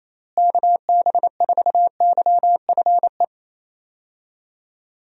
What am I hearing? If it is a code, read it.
Morse code, heard as a keyed tone, about 28 wpm, 700 Hz. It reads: KB4YFE
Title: USA Call Signs - 6 Characters - Repeated 2x - 28wpm